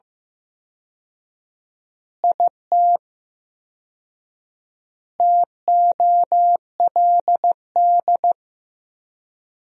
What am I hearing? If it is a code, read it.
Morse code, heard as a keyed tone, about 15 wpm, 700 Hz. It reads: IT TOLD
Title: 4x Word-Spacing - Sentences from Top 400 Words - 15wpm